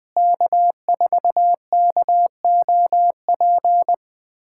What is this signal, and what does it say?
Morse code, heard as a keyed tone, about 20 wpm, 700 Hz. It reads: K4KOP
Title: USA Callsign DE Callsign - 5 Characters - 20wpm